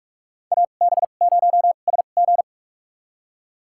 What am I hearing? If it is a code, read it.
Morse code, heard as a keyed tone, about 44 wpm, spaced as for 32 wpm, 700 Hz. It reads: AB0SG
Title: USA Call Signs - 2x Character Spacing - 45wpm